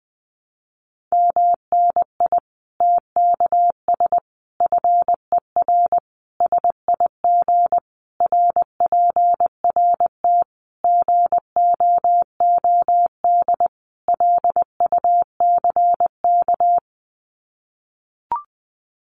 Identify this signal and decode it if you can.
Morse code, heard as a keyed tone, about 20 wpm, 700 Hz. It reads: MNI TKS FER SIG RPRT GOOD LUCK
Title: Common QSO Phrases 20wpm